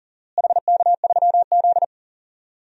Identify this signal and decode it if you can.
Morse code, heard as a keyed tone, about 40 wpm, 700 Hz. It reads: HK3Z
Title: DXCC-100 Call Signs - 4 Characters - Repeated 2x - 40wpm